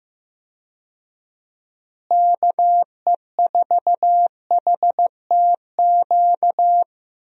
Morse code, heard as a keyed tone, about 15 wpm, 700 Hz. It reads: KE4HTQ
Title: USA Call Signs - 6 Characters - Repeated 2x - 15wpm